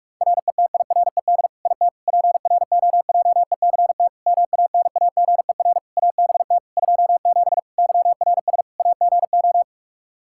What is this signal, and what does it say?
Morse code, heard as a keyed tone, about 46 wpm, 700 Hz. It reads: RETIRED IT PROJECT MANAGER ABT 27 YRS AGO
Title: Practice QSO - Exchange 3 - More Info - 45wpm